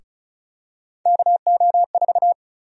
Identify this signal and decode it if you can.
Morse code, heard as a keyed tone, about 34 wpm, 700 Hz. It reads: KO4
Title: USA Call Sign Prefixes - 35wpm